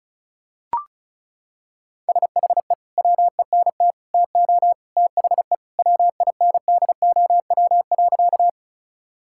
Morse code, heard as a keyed tone, about 34 wpm, 700 Hz. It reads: SHE WENT TO THE WINDOW.
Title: Sentences from Top 1600 Words 35wpm